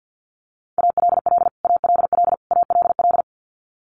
Morse code, heard as a keyed tone, about 50 wpm, 700 Hz. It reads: ALL ALL ALL
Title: Top 500 Words Repeated 50wpm